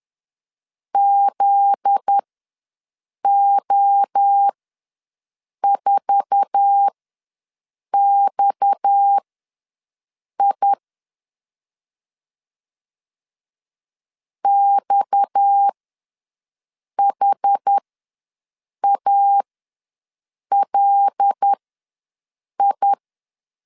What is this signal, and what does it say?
Morse code, heard as a keyed tone, about 10 wpm, spaced as for 6 wpm, 800 Hz. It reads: ZO4XI XHALI
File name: LEZIONI / 8 YJ45 / 8 YJ4 ws 10wpm 4020